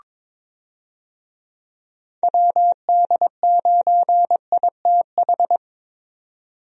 Morse code, heard as a keyed tone, about 21 wpm, 700 Hz. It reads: WD9ITH
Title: USA Call Signs - 6 Characters - 22wpm